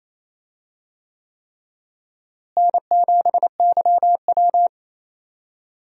Morse code, heard as a keyed tone, about 28 wpm, 700 Hz. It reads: N7YW
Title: USA Call Signs - 4 Characters - Repeated 3x - 28wpm